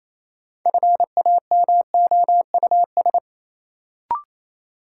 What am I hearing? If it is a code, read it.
Morse code, heard as a keyed tone, about 27 wpm, 700 Hz. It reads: FAMOUS
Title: Sentences from Top 1500 Words - New Word Drill Alt - 28wpm